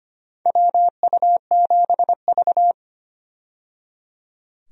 Morse code, heard as a keyed tone, about 25 wpm, 700 Hz. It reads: WU7V
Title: USA Call Signs - 4 Characters - 25wpm